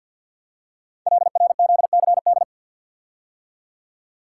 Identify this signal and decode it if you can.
Morse code, heard as a keyed tone, about 48 wpm, 700 Hz. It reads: LRZCD
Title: Code Groups - 5 Letters - 50wpm